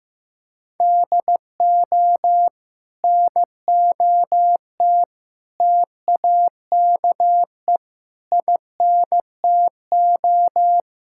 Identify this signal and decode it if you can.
Morse code, heard as a keyed tone, about 16 wpm, 700 Hz. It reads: DO NOT TAKE INTO
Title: Sentences from Top 1600 Words 15wpm